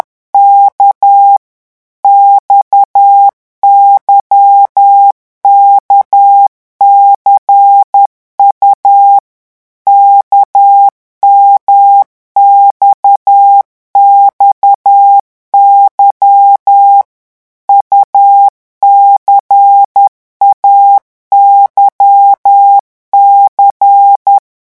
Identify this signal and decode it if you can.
Morse code, heard as a keyed tone, about 10 wpm, 800 Hz. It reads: K XYKCU KMXXY UCAYC
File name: morse mp3 / revision2-vitesse 10 mots